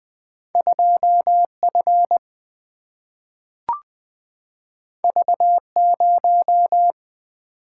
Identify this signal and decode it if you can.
Morse code, heard as a keyed tone, about 20 wpm, 700 Hz. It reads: 2F V0
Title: Single-Letter Single-Number 20wpm